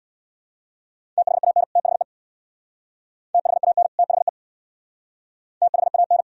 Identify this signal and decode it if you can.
Morse code, heard as a keyed tone, about 62 wpm, 700 Hz. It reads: THAN THE THAN THE THAN
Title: Speed-Racing - Sets of 2 Words 50wpm